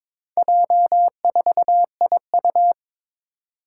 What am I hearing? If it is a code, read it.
Morse code, heard as a keyed tone, about 22 wpm, 700 Hz. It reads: J4IU
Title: Four Challenging Letter-Number Combinations 22wpm